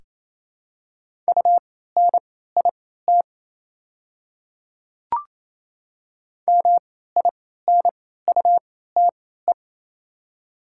Morse code, heard as a keyed tone, about 28 wpm, spaced as for 16 wpm, 700 Hz. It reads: UNIT MINUTE
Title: 3x Character Spacing - Top 500 Words - 28wpm